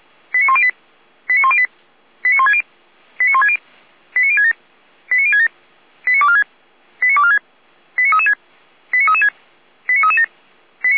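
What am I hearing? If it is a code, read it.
Five-tone selective call (ZVEI), heard as quick runs of five tones, tones of about 70 ms. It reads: 89108 89108 89160 89160 89067 89067 89246 89246 89336 89336 89337
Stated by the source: Five-tone ZVEI selective call 89099 received by SDR, from audio.f85360.feuerwehr-orlamuende.org fire brigade server.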